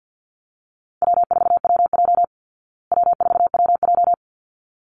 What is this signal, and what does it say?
Morse code, heard as a keyed tone, about 50 wpm, 700 Hz. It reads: W4WJ W4WJ
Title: Easy Call Signs Repeated 2x - 50wpm